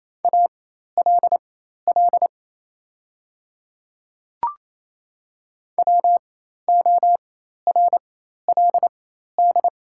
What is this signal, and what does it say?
Morse code, heard as a keyed tone, about 28 wpm, spaced as for 13 wpm, 700 Hz. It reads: ALL WORLD
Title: 4x Character Spacing - Top 200 Words - 28wpm